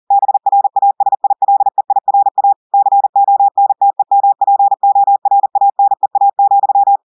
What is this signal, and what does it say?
Morse code, heard as a keyed tone, about 40 wpm, 800 Hz. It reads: BRASILEIRA CONTEMPORANEA,
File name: audios / portugais / portugais texte en clair 1 vitesse 40 mots minutes